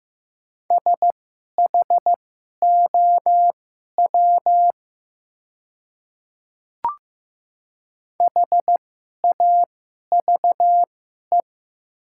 Morse code, heard as a keyed tone, about 15 wpm, spaced as for 11 wpm, 700 Hz. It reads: SHOW HAVE
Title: 2x Character Spacing - Top 200 Words - 15wpm